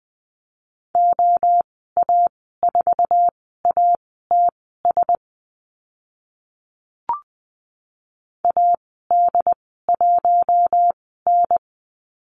Morse code, heard as a keyed tone, about 20 wpm, spaced as for 14 wpm, 700 Hz. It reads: OA4ATS AD1N
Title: 076 Lesson 13g - Letter L - Call Signs 20wpm